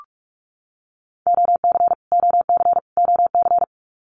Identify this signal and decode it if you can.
Morse code, heard as a keyed tone, about 44 wpm, 700 Hz. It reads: OC OC OC
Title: Abbreviations Repeated 3x - 45wpm